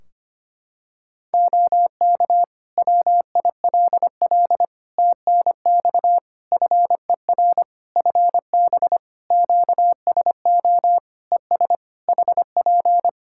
Morse code, heard as a keyed tone, about 25 wpm, 700 Hz. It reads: OK WILL TNX FER FB QSO ES HP
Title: Practice QSO - Exchange 4 - Ending - 25wpm